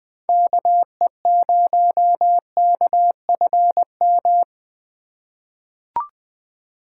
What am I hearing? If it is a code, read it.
Morse code, heard as a keyed tone, about 20 wpm, 700 Hz. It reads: KE0KFM
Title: USA Call Signs - 6 Characters - 4x Character Spacing - 20wpm